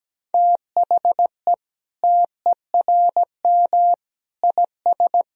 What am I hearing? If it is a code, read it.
Morse code, heard as a keyed tone, about 17 wpm, 700 Hz. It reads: THE TERM IS
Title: Sentences from Top 1000 Words - Review Part 4 - 17wpm